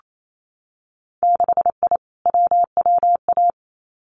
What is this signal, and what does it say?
Morse code, heard as a keyed tone, about 28 wpm, 690 Hz. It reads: 6I WWA
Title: Winter Field Day Exchanges Encore 28wpm